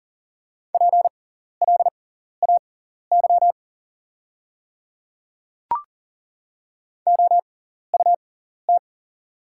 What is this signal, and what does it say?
Morse code, heard as a keyed tone, about 40 wpm, spaced as for 14 wpm, 690 Hz. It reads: PLAY OUT
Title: 6x Character Spacing - Top 300 Words - 40wpm